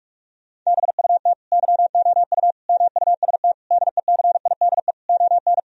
Morse code, heard as a keyed tone, about 41 wpm, 700 Hz. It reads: BUT YOU MUST DECIDE ON
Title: Sentences from Top 1000 Words - Review Part 3 - 45wpm